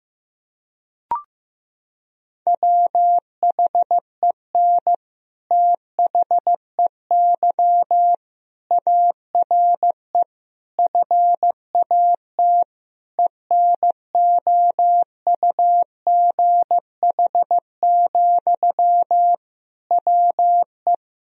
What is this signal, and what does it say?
Morse code, heard as a keyed tone, about 15 wpm, 700 Hz. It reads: WHEN THEY ARE FAT ENOUGH, WE
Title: Sentences from Top 1000 Words - Review Part 4 - 15wpm